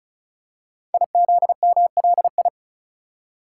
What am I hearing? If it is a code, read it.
Morse code, heard as a keyed tone, about 36 wpm, 690 Hz. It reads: IZMLI
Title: Code Groups - 5 Letters - 35wpm